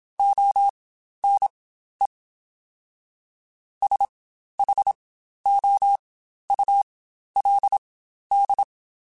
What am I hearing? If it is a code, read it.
Morse code, heard as a keyed tone, about 27 wpm, spaced as for 12 wpm, 800 Hz. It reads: ONE SHOULD
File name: Spaced / Chuckles Spaced-mp3s / BrainMouth sp @25